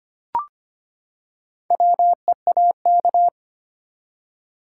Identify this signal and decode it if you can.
Morse code, heard as a keyed tone, about 25 wpm, 700 Hz. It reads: WEAK
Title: Four Letter Words - 25wpm